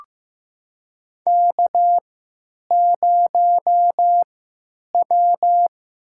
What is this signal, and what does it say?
Morse code, heard as a keyed tone, about 15 wpm, spaced as for 8 wpm, 700 Hz. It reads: K0W
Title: USA Call Signs - 5 Characters - 3x Character Spacing - 15wpm